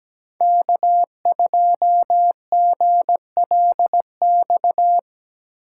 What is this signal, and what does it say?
Morse code, heard as a keyed tone, about 17 wpm, 700 Hz. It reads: K2GLX
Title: USA Call Signs - 2x Character Spacing - 17wpm